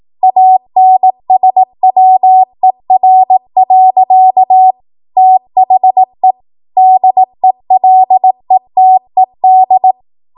Morse code, heard as a keyed tone, about 18 wpm, 750 Hz. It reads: ANSWER. THE DELETED